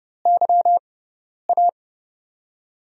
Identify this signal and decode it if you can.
Morse code, heard as a keyed tone, about 30 wpm, 700 Hz. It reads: Y A
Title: USA Call Signs - 6 Characters - 6x Character Spacing - 30wpm